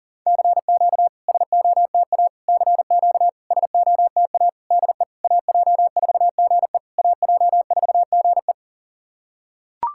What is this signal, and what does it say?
Morse code, heard as a keyed tone, about 40 wpm, 700 Hz. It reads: CQ SOTA CQ SOTA DE AJ4GE AJ4GE
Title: SOTA QSO Practice 40wpm